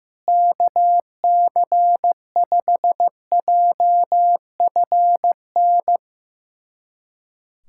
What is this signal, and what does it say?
Morse code, heard as a keyed tone, about 16 wpm, 700 Hz. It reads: KC5JFN
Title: USA Callsign DE Callsign - 6 Characters - 15wpm